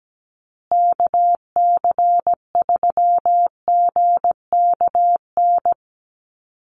Morse code, heard as a keyed tone, about 17 wpm, 700 Hz. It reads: KC3GKN
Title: USA Call Signs Repeated 2x 17wpm